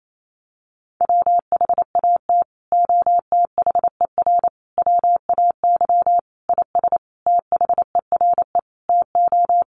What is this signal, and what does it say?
Morse code, heard as a keyed tone, about 28 wpm, 700 Hz. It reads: WHAT OTHER WAY IS THERE TO
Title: Sentences from Top 100 Words 28wpm